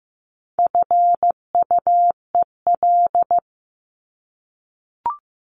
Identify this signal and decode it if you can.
Morse code, heard as a keyed tone, about 15 wpm, 700 Hz. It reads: FUEL
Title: Top 1751-2000 Words 15wpm